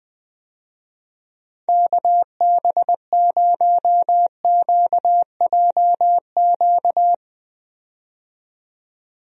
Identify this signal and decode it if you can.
Morse code, heard as a keyed tone, about 20 wpm, 700 Hz. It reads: KB0QJQ
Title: USA Call Signs - 6 Characters - 20wpm